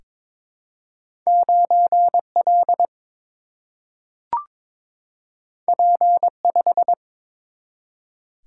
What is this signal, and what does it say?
Morse code, heard as a keyed tone, about 22 wpm, 700 Hz. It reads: 9L P5
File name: practice / 22wpm / Two Letter-Number Combinations - 22wpm